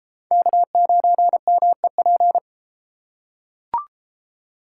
Morse code, heard as a keyed tone, about 32 wpm, 700 Hz. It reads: K9MEP